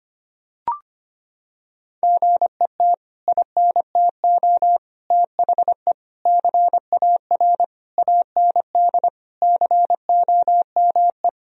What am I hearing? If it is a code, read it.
Morse code, heard as a keyed tone, about 25 wpm, 700 Hz. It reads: GET INTO THE CAR AND COME
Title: Sentences from Top 1000 Words - Review Part 1 - 25wpm